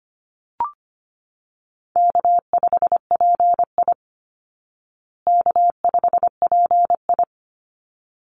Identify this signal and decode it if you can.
Morse code, heard as a keyed tone, about 25 wpm, 700 Hz. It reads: K5PI K5PI
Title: Easy Call Signs Repeated 2x - 25wpm